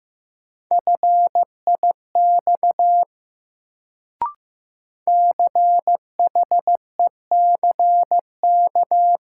Morse code, heard as a keyed tone, about 15 wpm, 700 Hz. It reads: FIX CHECK
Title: Sentences from Top 1400 Words - New Word Drill Alt - 15wpm